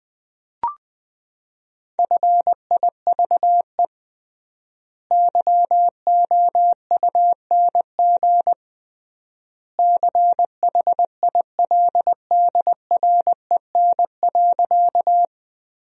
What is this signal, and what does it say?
Morse code, heard as a keyed tone, about 20 wpm, 700 Hz. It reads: FIVE YOUNG CHILDREN.
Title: 3x Word-Spacing - Sentences from Top 400 Words - 20wpm